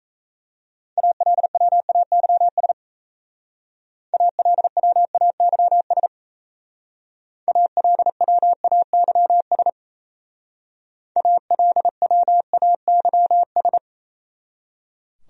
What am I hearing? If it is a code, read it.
Morse code, heard as a keyed tone, about 34 wpm, 690 Hz. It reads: ALWAYS ALWAYS ALWAYS ALWAYS
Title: Speed-Racing Top 500 Words 28wpm